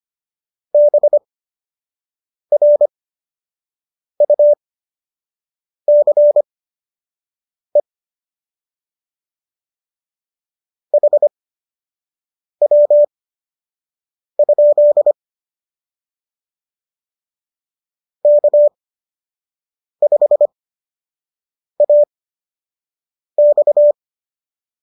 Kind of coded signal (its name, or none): Morse code